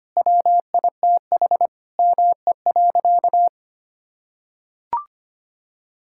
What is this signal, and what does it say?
Morse code, heard as a keyed tone, about 25 wpm, 700 Hz. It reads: WITH ME.
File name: practice / 25wpm / Sentences with 9-Word Phrases - 25wpm